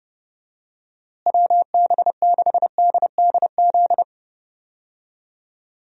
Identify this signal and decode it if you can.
Morse code, heard as a keyed tone, about 30 wpm, 700 Hz. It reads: WB6DDZ
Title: USA Call Signs - 6 Characters - Repeated 3x - 30wpm